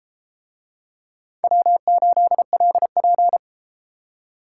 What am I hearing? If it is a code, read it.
Morse code, heard as a keyed tone, about 33 wpm, 700 Hz. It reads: W8LP